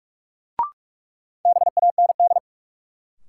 Morse code, heard as a keyed tone, about 41 wpm, 700 Hz. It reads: BAND
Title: Sentences from Top 1600 Words - New Word Drill - 45wpm